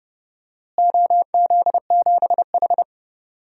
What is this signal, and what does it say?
Morse code, heard as a keyed tone, about 30 wpm, 700 Hz. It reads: OZ7H
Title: DXCC-100 Call Signs - 4 Characters - Repeated 2x - 30wpm